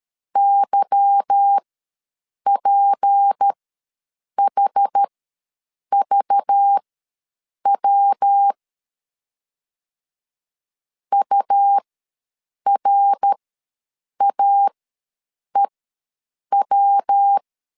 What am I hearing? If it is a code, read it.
Morse code, heard as a keyed tone, about 13 wpm, spaced as for 7 wpm, 800 Hz. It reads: YPHVW URAEW
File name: LEZIONI / 8 YJ45 / 8 Y ws 12wpm 3409